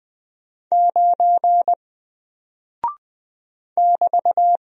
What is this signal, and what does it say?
Morse code, heard as a keyed tone, about 20 wpm, 700 Hz. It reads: 9 <BT>
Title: Single Character (Letter:Number:Punctuation:Prosign) 20wpm